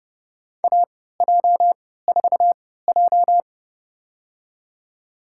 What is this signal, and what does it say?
Morse code, heard as a keyed tone, about 30 wpm, spaced as for 17 wpm, 700 Hz. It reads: AJ4J
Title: USA Call Signs - 4 Characters - 3x Character Spacing - 30wpm